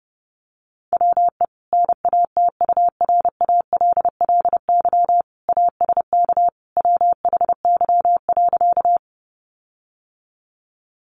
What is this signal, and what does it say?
Morse code, heard as a keyed tone, about 30 wpm, 700 Hz. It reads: WE NATURALLY ASK WHY.
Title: Sentences from Top 1300 Words 30wpm